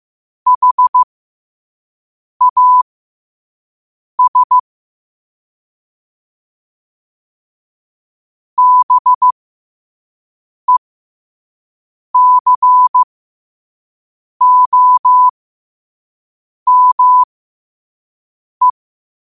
Morse code, heard as a keyed tone, about 15 wpm, spaced as for 5 wpm, 1000 Hz. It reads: HAS BECOME
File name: qotdfiles / 5wpm